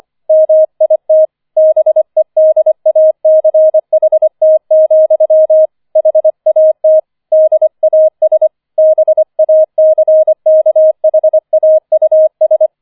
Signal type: Morse code